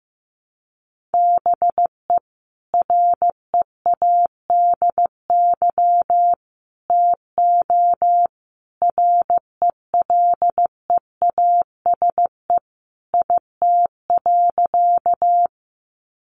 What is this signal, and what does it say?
Morse code, heard as a keyed tone, about 15 wpm, 700 Hz. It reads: BE READY TO RELEASE IT.
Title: Sentences from Top 1100 Words 15wpm